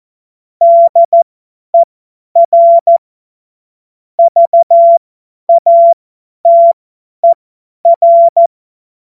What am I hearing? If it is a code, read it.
Morse code, heard as a keyed tone, about 14 wpm, spaced as for 10 wpm, 680 Hz. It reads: DER VATER
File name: Morsekurs / Witze / Witz 07. Dezember 2021, 50 F70